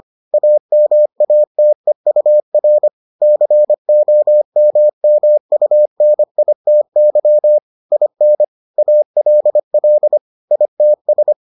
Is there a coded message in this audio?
Yes: Morse code from 0.3 to 11.3 s, heard as a keyed tone, about 25 wpm, 600 Hz. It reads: AMATEUR COMMUNITY IN ALL ITS